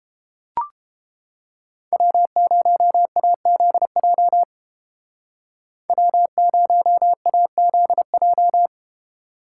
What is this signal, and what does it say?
Morse code, heard as a keyed tone, about 29 wpm, 700 Hz. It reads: W0AZJ W0AZJ